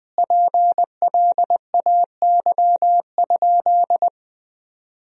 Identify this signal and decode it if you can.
Morse code, heard as a keyed tone, about 20 wpm, 700 Hz. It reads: PLAY?
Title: 2x Word-Spacing - Sentences from Top 300 Words - 20wpm